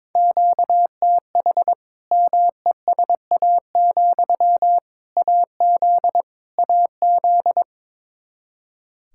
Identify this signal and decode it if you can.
Morse code, heard as a keyed tone, about 22 wpm, 700 Hz. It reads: QTH MESA, AZ AZ
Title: Practice QSO - Exchange 1 - Beginning - 22wpm